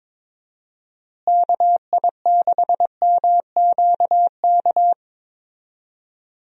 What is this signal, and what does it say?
Morse code, heard as a keyed tone, about 22 wpm, 700 Hz. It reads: KI6MQK
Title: USA Call Signs - 6 Characters - Repeated 3x - 22wpm